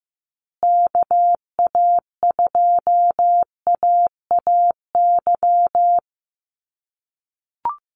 Morse code, heard as a keyed tone, about 15 wpm, 700 Hz. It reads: KA2AAY
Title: USA Call Signs Repeated 3x 15wpm